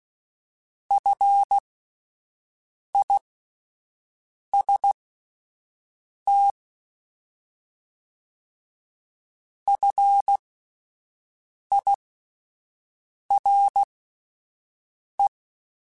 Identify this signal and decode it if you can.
Morse code, heard as a keyed tone, about 16 wpm, spaced as for 5 wpm, 800 Hz. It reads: FIST FIRE
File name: Alphabetical / Alphabetical mp3s / Fs @5